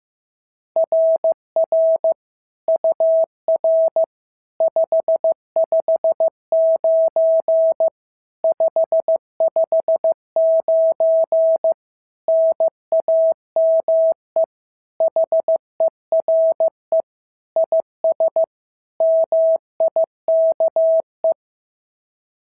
Morse code, heard as a keyed tone, about 14 wpm, 650 Hz. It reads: RR UR 559 559 NAME HERE IS MIKE
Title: SOTA QSO Practice 15wpm